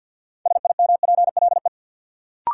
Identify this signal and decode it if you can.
Morse code, heard as a keyed tone, about 46 wpm, 700 Hz. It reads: SIMPLE
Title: Sentences from Top 400 Words - New Word Drill - 50wpm